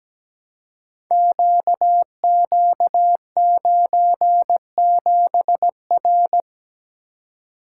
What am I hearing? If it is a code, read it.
Morse code, heard as a keyed tone, about 17 wpm, 700 Hz. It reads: QQ97R
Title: Code Groups - 5 Letter:Numbers - 17wpm